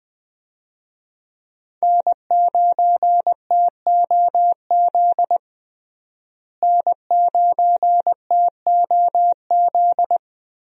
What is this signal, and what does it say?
Morse code, heard as a keyed tone, about 20 wpm, 700 Hz. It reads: N9TOZ N9TOZ
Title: USA Call Signs - 5 Characters - Repeated 3x - 20wpm